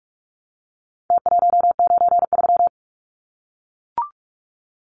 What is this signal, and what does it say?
Morse code, heard as a keyed tone, about 44 wpm, 700 Hz. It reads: T193
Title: Numbers - Contest Style - 4-Digit 45wpm